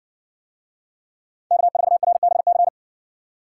Morse code, heard as a keyed tone, about 60 wpm, 700 Hz. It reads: K4RBC